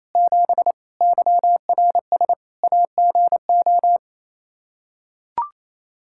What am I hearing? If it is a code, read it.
Morse code, heard as a keyed tone, about 28 wpm, 700 Hz. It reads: 7 YRS AGO
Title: Practice QSO - Exchange 3 - More Info - 28wpm